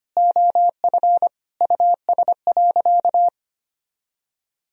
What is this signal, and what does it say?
Morse code, heard as a keyed tone, about 25 wpm, 700 Hz. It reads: OF US.
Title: Sentences from Top 300 Words and Abbreviations 25wpm